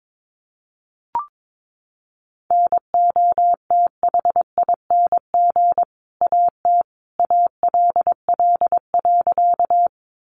Morse code, heard as a keyed tone, about 22 wpm, 700 Hz. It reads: NOTHING AT ALL.
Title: Sentences from Top 500 Words 22wpm